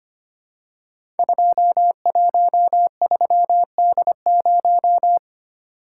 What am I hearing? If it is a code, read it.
Morse code, heard as a keyed tone, about 25 wpm, 700 Hz. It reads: 213D0
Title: Code Groups - 5 Letter:Numbers - 25wpm